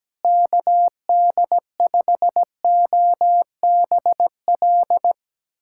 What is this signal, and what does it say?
Morse code, heard as a keyed tone, about 17 wpm, 700 Hz. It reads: KD5OBL
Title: USA Callsign DE Callsign 17wpm